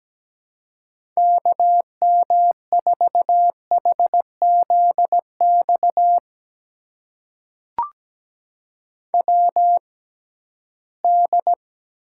Morse code, heard as a keyed tone, about 17 wpm, spaced as for 13 wpm, 700 Hz. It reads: KM4HZX WD
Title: USA Call Signs - 6 Characters - 6x Character Spacing - 17wpm